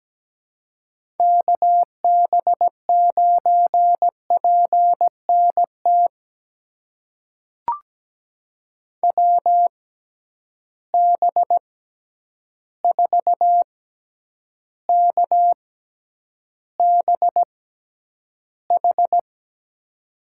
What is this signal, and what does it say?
Morse code, heard as a keyed tone, about 17 wpm, spaced as for 9 wpm, 700 Hz. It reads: KB9PNT WB4KBH
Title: USA Call Signs - 6 Characters - 6x Character Spacing - 17wpm